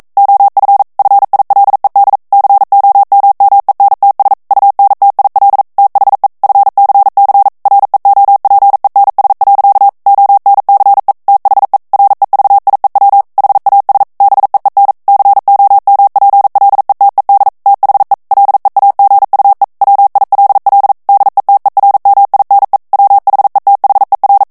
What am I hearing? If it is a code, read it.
Morse code, heard as a keyed tone, about 42 wpm, 800 Hz. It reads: OF FILED COMMENTS UNTIL THE FCC REOPENS. ONCE THE REVIEW HAS BEEN COMPLETED THE LEAGUE WILL DETERMINE WHETHER